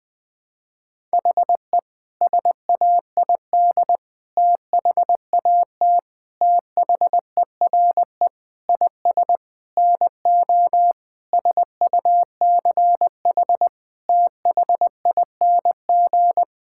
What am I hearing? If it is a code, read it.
Morse code, heard as a keyed tone, about 20 wpm, 700 Hz. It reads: HE SAID THAT THERE IS NO SUCH THING
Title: Sentences with 8-Word Phrases - Spoken Beginning - 20wpm